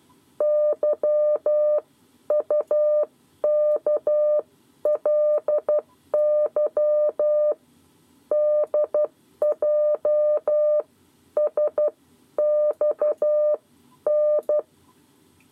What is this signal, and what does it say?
Morse code, heard as a keyed tone, about 12 wpm, spaced as for 10 wpm, 600 Hz. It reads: YUKLY DJSXN
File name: 7 Lezione cw 2